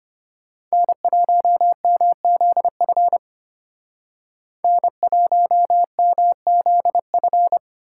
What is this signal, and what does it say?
Morse code, heard as a keyed tone, about 33 wpm, 700 Hz. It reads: N1MZF N1MZF